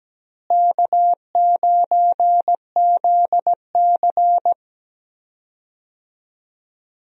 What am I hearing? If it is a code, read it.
Morse code, heard as a keyed tone, about 17 wpm, 700 Hz. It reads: K9ZC
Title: USA Call Signs - 4 Characters - Repeated 3x - 17wpm